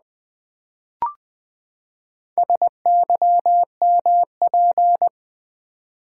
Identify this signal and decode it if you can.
Morse code, heard as a keyed tone, about 20 wpm, 700 Hz. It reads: SYMP
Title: Build A Word - 4 Letters At A Time - 8-Letter Words - 20wpm